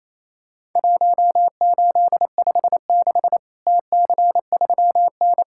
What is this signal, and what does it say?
Morse code, heard as a keyed tone, about 28 wpm, 700 Hz. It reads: 1856 TC3N
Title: Worked All Europe DX Contest - QTC Exchange - 28wpm